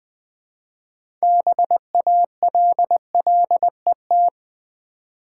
Morse code, heard as a keyed tone, about 20 wpm, 700 Hz. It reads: BALLET
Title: 6 Letter Words - First 3 Letters Given - 20wpm